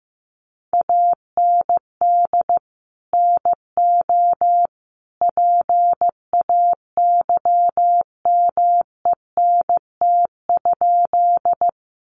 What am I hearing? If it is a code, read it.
Morse code, heard as a keyed tone, about 15 wpm, 700 Hz. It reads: AND NO PAYMENT?
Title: Sentences from Top 1200 Words 15wpm